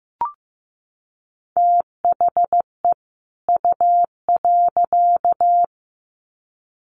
Morse code, heard as a keyed tone, about 15 wpm, 700 Hz. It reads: THE U.
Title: Sets of 2 Words 15wpm